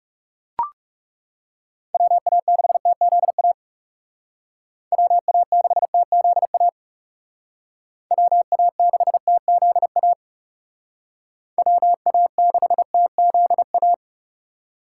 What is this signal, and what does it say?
Morse code, heard as a keyed tone, about 35 wpm, 700 Hz. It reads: WA6TZA WA6TZA WA6TZA WA6TZA
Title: Speed Racing - USA Call Signs 30wpm